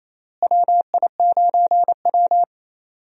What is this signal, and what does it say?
Morse code, heard as a keyed tone, about 28 wpm, 700 Hz. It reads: WI9W
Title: USA Call Signs - 4 Characters - Repeated 3x - 28wpm